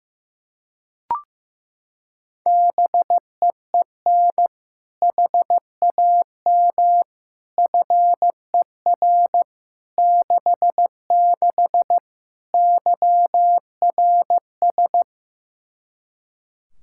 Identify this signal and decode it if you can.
Morse code, heard as a keyed tone, about 15 wpm, 700 Hz. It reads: BEEN HAM FER 66 YRS
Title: Practice QSO - Exchange 3 - More Info - 15wpm